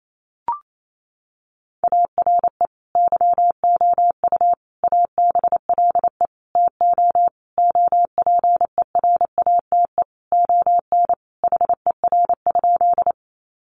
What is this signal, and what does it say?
Morse code, heard as a keyed tone, about 28 wpm, 700 Hz. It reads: ARE YOU ABLE TO OPERATE ON HER?
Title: Sentences from Top 1500 Words 28wpm